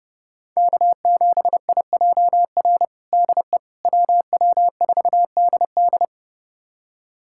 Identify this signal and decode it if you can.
Morse code, heard as a keyed tone, about 30 wpm, 700 Hz. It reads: K7IJR DE WW4DD
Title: USA Callsign DE Callsign - 5 Characters - 30wpm